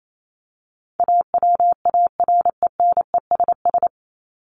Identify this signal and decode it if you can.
Morse code, heard as a keyed tone, about 28 wpm, 700 Hz. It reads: AWARENESS
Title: Top 1501-1750 Words 28wpm